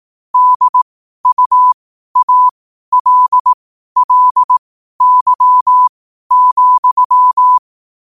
Morse code, heard as a keyed tone, about 18 wpm, spaced as for 12 wpm, 1000 Hz. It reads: DUALLY,
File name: qotdfiles / 12wpm